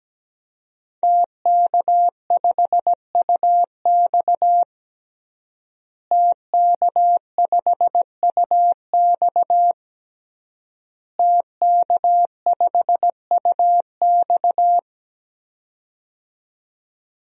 Morse code, heard as a keyed tone, about 17 wpm, 700 Hz. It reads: TK5UX TK5UX TK5UX
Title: DXCC-100 Call Signs - 5 Characters - Repeated 3x - 17wpm